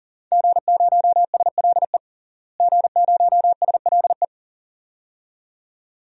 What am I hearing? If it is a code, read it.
Morse code, heard as a keyed tone, about 40 wpm, 700 Hz. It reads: G0SLE G0SLE
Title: DXCC-100 Call Signs - 5 Characters - Repeated 2x - 40wpm